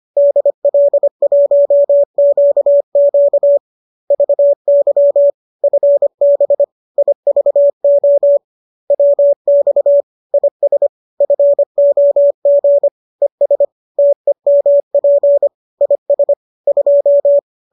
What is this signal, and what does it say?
Morse code, heard as a keyed tone, about 25 wpm, 570 Hz. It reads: DL1QQ VY FB IVO WX IS FOG ES TEMP IS 2
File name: qso219 25